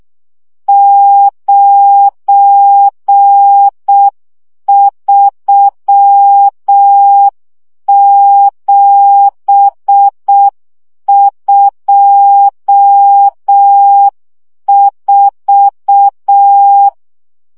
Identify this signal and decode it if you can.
Morse code, heard as a keyed tone, about 6 wpm, 800 Hz. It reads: 93724
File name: morze vizsga 2011-2 5